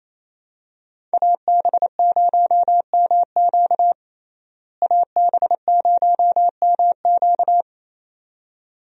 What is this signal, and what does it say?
Morse code, heard as a keyed tone, about 28 wpm, 700 Hz. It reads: AB0MQ AB0MQ
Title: USA Call Signs - 5 Characters - Repeated 2x - 28wpm